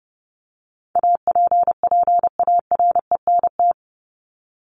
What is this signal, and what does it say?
Morse code, heard as a keyed tone, about 30 wpm, 700 Hz. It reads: APPARENT
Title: LY Suffix - Top 100 Words with Base Words 30wpm